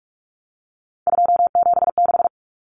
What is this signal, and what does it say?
Morse code, heard as a keyed tone, about 44 wpm, 700 Hz. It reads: 276
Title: Numbers - 3-Digit 45wpm